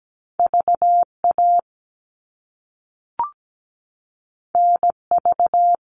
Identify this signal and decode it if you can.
Morse code, heard as a keyed tone, about 17 wpm, 700 Hz. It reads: VA NV
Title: All Cut Numbers - 2-Digit 17wpm